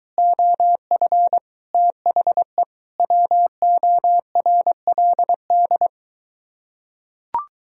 Morse code, heard as a keyed tone, about 23 wpm, 700 Hz. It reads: OF THE WORLD